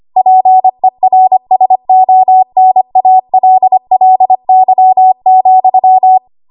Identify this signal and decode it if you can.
Morse code, heard as a keyed tone, about 25 wpm, 750 Hz. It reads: PERSONALLY,
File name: Archive / 25 WPM / 25WPM